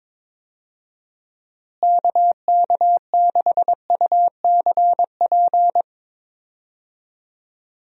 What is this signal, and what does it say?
Morse code, heard as a keyed tone, about 22 wpm, 700 Hz. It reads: KK6UCP
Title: USA Call Signs - 6 Characters - Repeated 2x - 22wpm